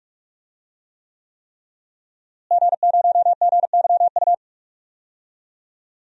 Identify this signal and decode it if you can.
Morse code, heard as a keyed tone, about 46 wpm, 700 Hz. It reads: G0GYU
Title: DXCC-100 Call Signs - 5 Characters - Repeated 3x - 45wpm